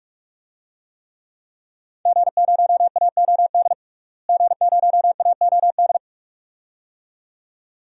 Morse code, heard as a keyed tone, about 46 wpm, 700 Hz. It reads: G0AOD G0AOD
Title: DXCC-100 Call Signs - 5 Characters - Repeated 2x - 45wpm